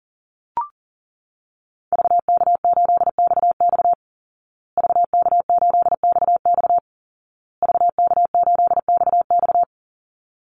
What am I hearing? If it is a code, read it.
Morse code, heard as a keyed tone, about 40 wpm, 700 Hz. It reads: VK8XX VK8XX VK8XX
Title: Call Signs Repeated 2x - 40wpm